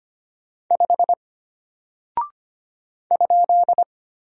Morse code, heard as a keyed tone, about 25 wpm, 700 Hz. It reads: 5 ?
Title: Single Character (Letter:Number:Punctuation:Prosign) Alt 25wpm